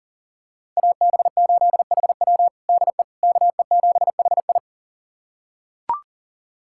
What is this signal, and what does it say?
Morse code, heard as a keyed tone, about 39 wpm, 700 Hz. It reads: AB8HW DE KE7HI
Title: USA Callsign DE Callsign - 5 Characters - 40wpm